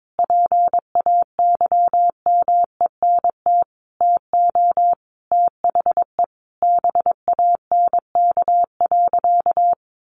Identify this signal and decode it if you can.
Morse code, heard as a keyed tone, about 22 wpm, 700 Hz. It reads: PAYMENT TO THE BANK.
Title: Sentences from Top 1500 Words 22wpm